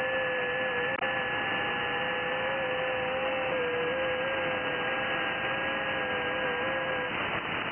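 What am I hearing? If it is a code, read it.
FT8.
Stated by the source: FT8 audio from an Icom IC-7100 radio.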